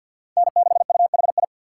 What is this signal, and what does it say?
Morse code, heard as a keyed tone, about 47 wpm, 700 Hz. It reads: N6UHI
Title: USA Call Signs - 5 Characters - 6x Character Spacing - 50wpm